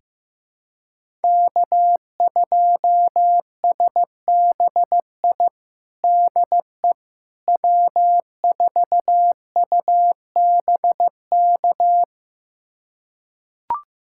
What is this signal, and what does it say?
Morse code, heard as a keyed tone, about 15 wpm, 700 Hz. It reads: K2SBI DE W4UBK
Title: USA Callsign DE Callsign - 5 Characters - 15wpm